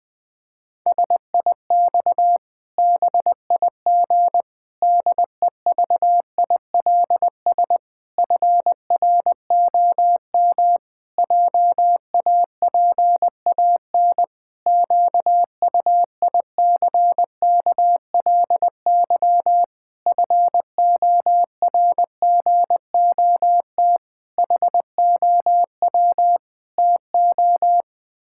Morse code, heard as a keyed tone, about 20 wpm, 700 Hz. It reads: SIX BIG DEVILS FROM JAPAN QUICKLY FORGOT HOW TO
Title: Pangram Challenge v2 - 20wpm